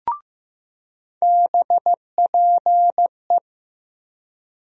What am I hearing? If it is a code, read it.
Morse code, heard as a keyed tone, about 15 wpm, 700 Hz. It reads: BPE
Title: USA Call Sign Postfixes - 3 Letters 15wpm